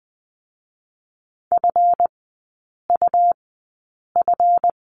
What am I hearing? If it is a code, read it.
Morse code, heard as a keyed tone, about 20 wpm, spaced as for 8 wpm, 700 Hz. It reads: FUF
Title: 138 Lesson 23e - Letter f - Three Characters 20wpm